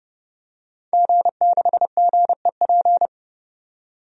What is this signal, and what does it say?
Morse code, heard as a keyed tone, about 30 wpm, 700 Hz. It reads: G6GEP
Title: DXCC-100 Call Signs - 5 Characters - 30wpm